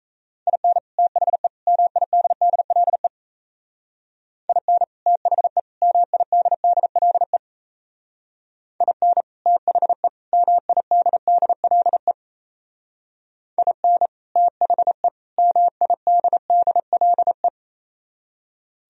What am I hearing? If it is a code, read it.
Morse code, heard as a keyed tone, about 36 wpm, 700 Hz. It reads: IN THE MIDDLE IN THE MIDDLE IN THE MIDDLE IN THE MIDDLE